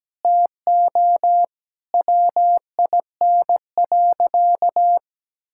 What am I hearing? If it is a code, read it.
Morse code, heard as a keyed tone, about 17 wpm, 700 Hz. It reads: TO WIN.
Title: Sentences from Top 1600 Words 17wpm